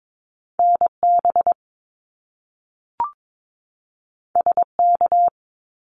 Morse code, heard as a keyed tone, about 22 wpm, 700 Hz. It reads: NB SK
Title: Canadian Provinces and Territories 22wpm